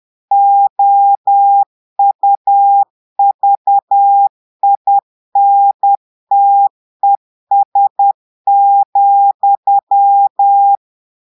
Morse code, heard as a keyed tone, about 10 wpm, 800 Hz. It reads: OUVINTES,
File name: portugais texte en clair 2 vitesse 10 mots minutes